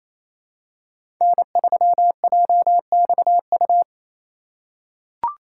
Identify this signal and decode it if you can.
Morse code, heard as a keyed tone, about 28 wpm, 700 Hz. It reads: N3JXU
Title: USA Call Signs - 5 Characters - 4x Character Spacing - 28wpm